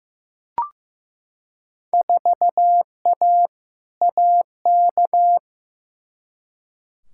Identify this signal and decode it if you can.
Morse code, heard as a keyed tone, about 15 wpm, 700 Hz. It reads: 4A AK
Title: Field Day Exchanges - 15wpm